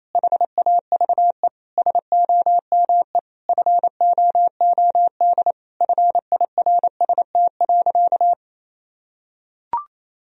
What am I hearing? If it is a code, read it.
Morse code, heard as a keyed tone, about 28 wpm, 700 Hz. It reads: HAVE SOME FOOD FIRST.
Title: Sentences from Top 1000 Words - Review Part 1 - 28wpm